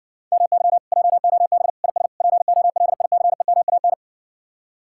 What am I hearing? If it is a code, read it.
Morse code, heard as a keyed tone, about 60 wpm, 700 Hz. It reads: MY JOB IS POLICEMAN